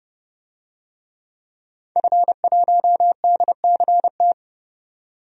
Morse code, heard as a keyed tone, about 29 wpm, 700 Hz. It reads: F1DCT
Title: DXCC-100 Call Signs - 5 Characters - 30wpm